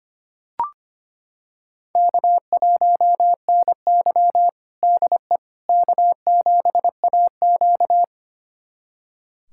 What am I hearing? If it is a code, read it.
Morse code, heard as a keyed tone, about 25 wpm, 700 Hz. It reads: K1NY DE K7AQ
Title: USA Callsign DE Callsign - 4 Characters - 25wpm